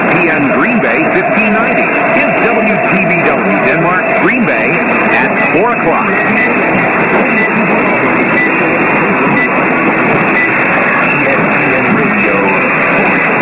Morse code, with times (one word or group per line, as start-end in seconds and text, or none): none